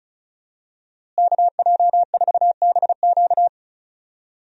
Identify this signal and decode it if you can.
Morse code, heard as a keyed tone, about 34 wpm, 700 Hz. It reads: KJ4BQ
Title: USA Call Signs - 5 Characters - Repeated 2x - 35wpm